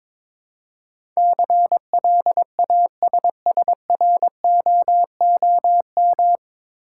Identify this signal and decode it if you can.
Morse code, heard as a keyed tone, about 22 wpm, 700 Hz. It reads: CLASSROOM
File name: 9 Letter Words - First 3 Letters Given - 22wpm